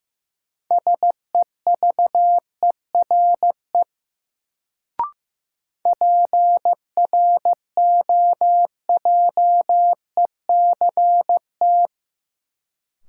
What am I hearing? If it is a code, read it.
Morse code, heard as a keyed tone, about 16 wpm, 700 Hz. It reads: SEVERE PROJECT
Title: Sentences from Top 1400 Words - New Word Drill - 15wpm